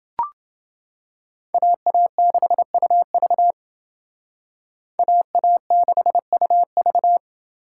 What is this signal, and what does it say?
Morse code, heard as a keyed tone, about 28 wpm, 700 Hz. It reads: AA6UV AA6UV